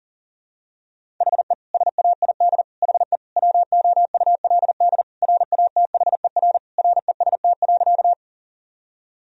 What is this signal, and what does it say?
Morse code, heard as a keyed tone, about 37 wpm, 700 Hz. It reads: HE SAID HE WOULD RATHER REST.
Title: Sentences from Top 1000 Words - Review Part 4 - 40wpm